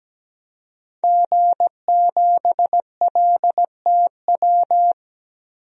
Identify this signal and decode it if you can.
Morse code, heard as a keyed tone, about 17 wpm, 700 Hz. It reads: G7LTW
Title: DXCC-100 Call Signs - 5 Characters - 17wpm